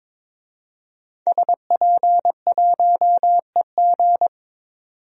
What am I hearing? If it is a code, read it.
Morse code, heard as a keyed tone, about 22 wpm, 700 Hz. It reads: SP1EG
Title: DXCC-100 Call Signs - 5 Characters - 22wpm